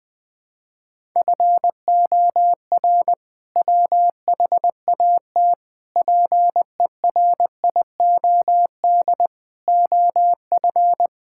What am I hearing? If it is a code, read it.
Morse code, heard as a keyed tone, about 20 wpm, 700 Hz. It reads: FOR WHAT PERIOD OF
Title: Sentences from Top 1000 Words - Review Part 5 - 20wpm